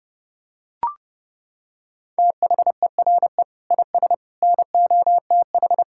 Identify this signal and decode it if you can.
Morse code, heard as a keyed tone, about 30 wpm, 700 Hz. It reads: THERE IS NOTH
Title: Sentences from Top 1600 Words 30wpm